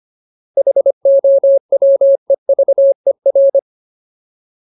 Morse code, heard as a keyed tone, about 25 wpm, 550 Hz. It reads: HOWEVER
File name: Ultimate Word List / 25WPM Ultimate